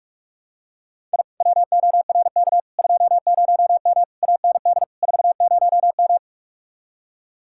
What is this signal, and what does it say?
Morse code, heard as a keyed tone, about 49 wpm, 700 Hz. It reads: I WORK 20M AND 40M